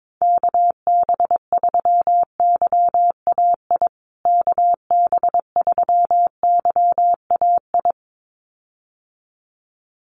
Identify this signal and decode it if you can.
Morse code, heard as a keyed tone, about 22 wpm, 700 Hz. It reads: KB3YAI KB3YAI
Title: USA Call Signs Repeated 2x 22wpm